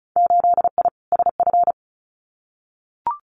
Morse code, heard as a keyed tone, about 36 wpm, 700 Hz. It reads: 8I SF
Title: Winter Field Day Exchanges Encore 35wpm